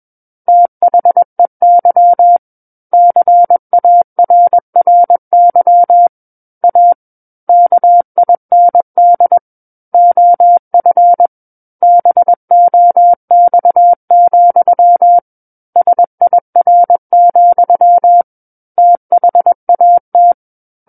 Morse code, heard as a keyed tone, about 21 wpm, 700 Hz. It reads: THEY CARRY A KIND OF BOX, SIR, THAT
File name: Book One / 12-Chapter 12 - 21 WPM